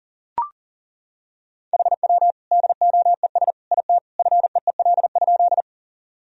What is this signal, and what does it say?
Morse code, heard as a keyed tone, about 37 wpm, 700 Hz. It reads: HW DOES IT FEEL?
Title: Sentences from Top 400 Words and Abbreviations 40wpm